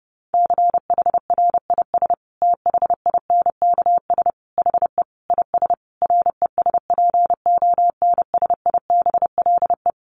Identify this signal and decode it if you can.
Morse code, heard as a keyed tone, about 28 wpm, 700 Hz. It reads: CHRIS THINKS HE IS RESPONSIBLE
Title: Easy Phrases 30wpm